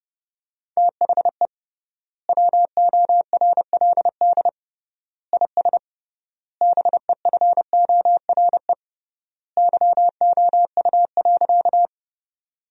Morse code, heard as a keyed tone, about 31 wpm, 700 Hz. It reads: THE WORLD IS BEFORE YOU.
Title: 3x Word-Spacing - Sentences from Top 200 Words - 30wpm